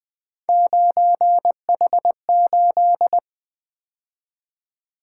Morse code, heard as a keyed tone, about 24 wpm, 700 Hz. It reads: 9H8
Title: Speed-Racing - Three Letter-Number Combinations - 20wpm